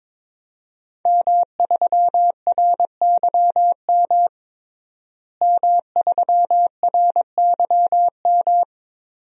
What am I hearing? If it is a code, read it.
Morse code, heard as a keyed tone, about 22 wpm, 700 Hz. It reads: M3RYM M3RYM
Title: DXCC-100 Call Signs - 5 Characters - Repeated 2x - 22wpm